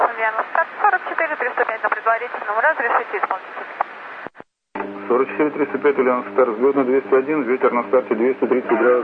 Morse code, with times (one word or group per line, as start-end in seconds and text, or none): none